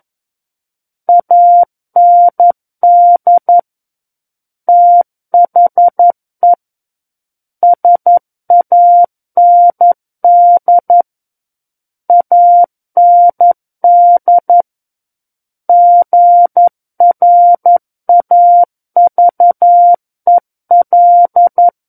Morse code, heard as a keyed tone, about 11 wpm, 700 Hz. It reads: AND THE SAND AND GRAVEL
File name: Book One / 02-Chapter 2 - 11 WPM